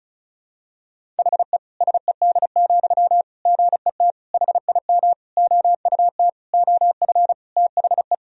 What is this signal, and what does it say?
Morse code, heard as a keyed tone, about 36 wpm, 700 Hz. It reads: HE SED, GET HIM OUT OF THE
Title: Sentences from Top 400 Words and Abbreviations 35wpm